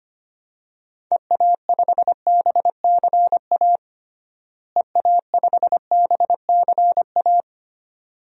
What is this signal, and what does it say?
Morse code, heard as a keyed tone, about 25 wpm, 700 Hz. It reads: EA5BCA EA5BCA
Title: DXCC-100 Call Signs - 6 Characters - Repeated 2x - 25wpm